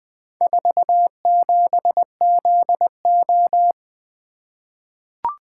Morse code, heard as a keyed tone, about 20 wpm, 700 Hz. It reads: 47ZO
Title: Four Letter-Number Combinations - 20wpm